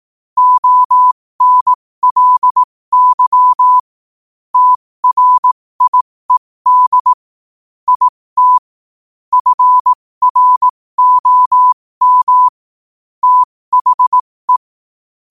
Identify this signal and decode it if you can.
Morse code, heard as a keyed tone, about 18 wpm, spaced as for 15 wpm, 1000 Hz. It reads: ONLY TRIED IT FROM THE
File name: qotdfiles / 15wpm